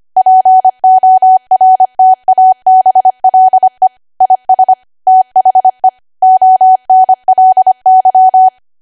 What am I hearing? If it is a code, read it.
Morse code, heard as a keyed tone, about 25 wpm, 750 Hz. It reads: PORTABLE IS THE ONLY
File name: Archive / 25 WPM / 25WPM